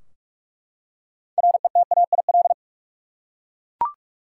Morse code, heard as a keyed tone, about 41 wpm, 700 Hz. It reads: RETAIL
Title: Practice QSO - Exchange 3 - More Info - 45wpm